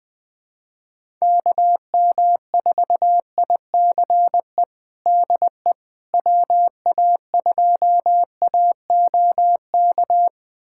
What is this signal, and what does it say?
Morse code, heard as a keyed tone, about 20 wpm, 700 Hz. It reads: KM4ICE DE WA2AOK
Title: USA Callsign DE Callsign - 6 Characters - 20wpm